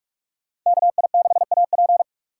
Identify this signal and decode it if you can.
Morse code, heard as a keyed tone, about 42 wpm, 700 Hz. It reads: KI6AP